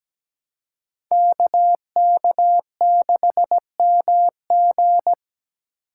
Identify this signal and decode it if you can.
Morse code, heard as a keyed tone, about 17 wpm, 700 Hz. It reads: KK6MG
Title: USA Call Signs - 5 Characters - 2x Character Spacing - 17wpm